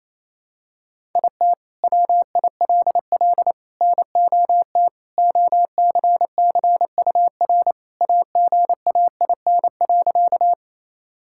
Morse code, heard as a keyed tone, about 28 wpm, 700 Hz. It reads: IT WILL NOT OCCUR AGAIN.
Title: Sentences from Top 1000 Words - Review Part 6 - 28wpm